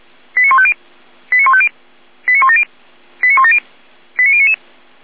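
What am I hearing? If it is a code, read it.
Five-tone selective call (ZVEI), heard as quick runs of five tones, tones of about 70 ms. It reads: 89150 89150 89170 89170 89099
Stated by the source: Five-tone ZVEI selective call received by SDR, code 89168.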